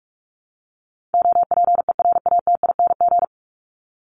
Morse code, heard as a keyed tone, about 44 wpm, 700 Hz. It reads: OPERATING
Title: ING Suffix - Top 100 Words with Base Words 45wpm